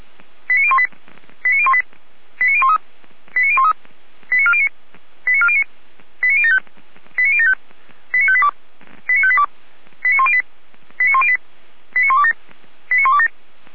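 Five-tone selective call (ZVEI), heard as quick runs of five tones, tones of about 70 ms. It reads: 89018 89018 89013 89013 89409 89409 89075 89075 89582 89582 89108 89108 89127 89127
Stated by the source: ZVEI five-tone selective call received by SDR.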